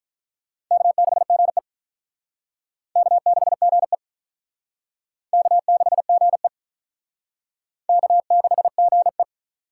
Morse code, heard as a keyed tone, about 47 wpm, 700 Hz. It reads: K6GE K6GE K6GE K6GE